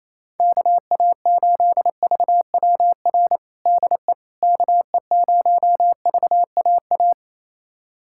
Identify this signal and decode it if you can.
Morse code, heard as a keyed tone, about 28 wpm, 700 Hz. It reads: KA8VWR DE KE0VAA
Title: USA Callsign DE Callsign - 6 Characters - 28wpm